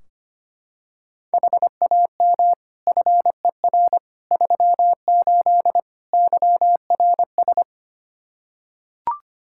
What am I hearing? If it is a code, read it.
Morse code, heard as a keyed tone, about 25 wpm, 700 Hz. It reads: HAM FER 38 YRS
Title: K1USN SST QSO Practice 25wpm